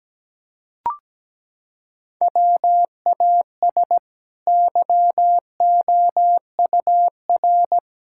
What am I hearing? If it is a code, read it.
Morse code, heard as a keyed tone, about 17 wpm, 700 Hz. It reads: WAS YOUR
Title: Sentences from Top 1000 Words - Review Part 5 - 17wpm